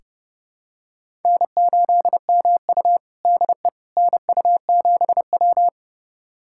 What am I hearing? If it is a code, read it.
Morse code, heard as a keyed tone, about 30 wpm, 700 Hz. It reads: N8MU DE NU7W
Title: USA Callsign DE Callsign - 4 Characters - 30wpm